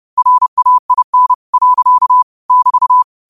Morse code, heard as a keyed tone, about 30 wpm, 1000 Hz. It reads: RAIN. <BT>
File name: qotdfiles / 30wpm